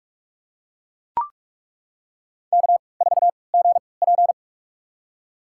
Morse code, heard as a keyed tone, about 42 wpm, spaced as for 25 wpm, 700 Hz. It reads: K4GP
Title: USA Call Signs - 4 Characters - 3x Character Spacing - 45wpm